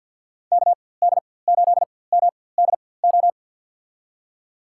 Morse code, heard as a keyed tone, about 52 wpm, spaced as for 23 wpm, 700 Hz. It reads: KD8MDO
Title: USA Call Signs - 6 Characters - 4x Character Spacing - 50wpm